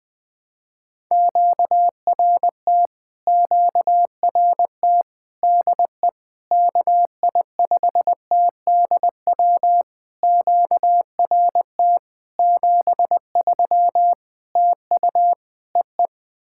Morse code, heard as a keyed tone, about 20 wpm, 700 Hz. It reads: QRT QRT DE KI5TDW QRT 73 TU EE
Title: POTA QSO Practice - 20wpm